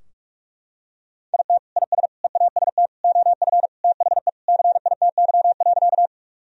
Morse code, heard as a keyed tone, about 41 wpm, 700 Hz. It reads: IT IS EAST OF THE CITY.
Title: Sentences from Top 1000 Words - Review Part 3 - 45wpm